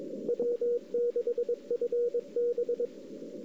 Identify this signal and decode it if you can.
Morse code, heard as a keyed tone, about 22 wpm, 480 Hz. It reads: W6FB
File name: CW / 2008-ss-cw-k5tr-op-n5rz -04:00